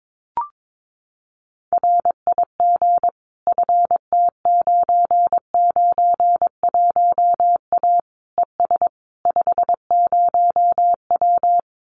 Morse code, heard as a keyed tone, about 22 wpm, 700 Hz. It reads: RIG FT991A ES 50W
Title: Common QSO Phrases Encore 22wpm